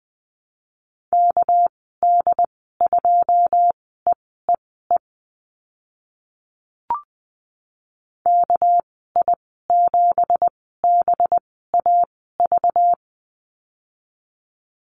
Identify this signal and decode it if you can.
Morse code, heard as a keyed tone, about 20 wpm, spaced as for 14 wpm, 700 Hz. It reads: KD2EEE KI7BAV
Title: 219 Lesson 35g - Number 0 - Call Signs 20wpm